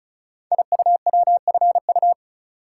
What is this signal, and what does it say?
Morse code, heard as a keyed tone, about 36 wpm, 700 Hz. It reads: IUWFU
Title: Code Groups - 5 Letters - 35wpm